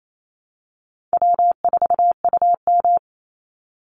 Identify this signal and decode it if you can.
Morse code, heard as a keyed tone, about 28 wpm, 700 Hz. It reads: W4UM
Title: Easy Call Signs Repeated 2x - 28wpm